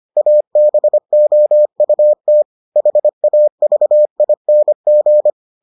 Morse code, heard as a keyed tone, about 25 wpm, 600 Hz. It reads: ABOUT HAVING